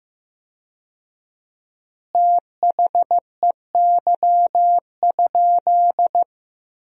Morse code, heard as a keyed tone, about 15 wpm, 700 Hz. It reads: THEY?
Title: 4x Word-Spacing - Sentences from Top 300 Words - 15wpm